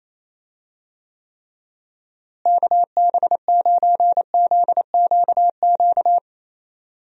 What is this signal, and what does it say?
Morse code, heard as a keyed tone, about 27 wpm, 700 Hz. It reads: KB9ZQQ
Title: USA Call Signs - 6 Characters - Repeated 2x - 28wpm